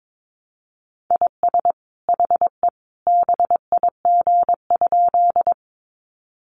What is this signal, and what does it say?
Morse code, heard as a keyed tone, about 22 wpm, 700 Hz. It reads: IS HE BIG?
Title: Sentences from Top 200 Words 22wpm